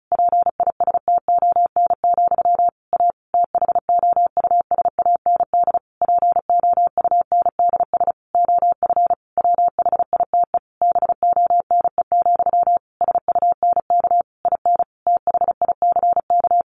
Morse code, heard as a keyed tone, about 34 wpm, 700 Hz. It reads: PISTON, A THOUSAND POUNDS OF WHITE BONE, SUNK IN THICK
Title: A Sound Of Thunder 35wpm